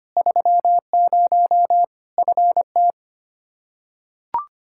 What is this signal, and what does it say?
Morse code, heard as a keyed tone, about 25 wpm, 700 Hz. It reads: 30 FT
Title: K1USN SST QSO Practice 25wpm